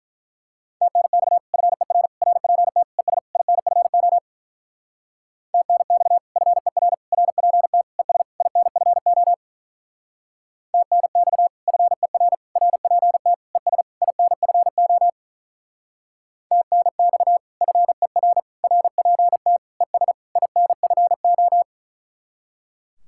Morse code, heard as a keyed tone, about 45 wpm, 700 Hz. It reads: TNX FER RPT ES INFO TNX FER RPT ES INFO TNX FER RPT ES INFO TNX FER RPT ES INFO